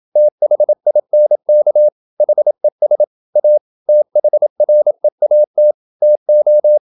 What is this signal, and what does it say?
Morse code, heard as a keyed tone, about 28 wpm, 600 Hz. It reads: THINK HES A THREAT TO
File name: Never 0027wpm